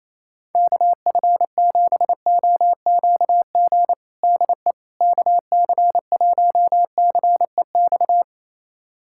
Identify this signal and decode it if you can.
Morse code, heard as a keyed tone, about 28 wpm, 700 Hz. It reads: KF7OQG DE KC1CEX
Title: USA Callsign DE Callsign - 6 Characters - 28wpm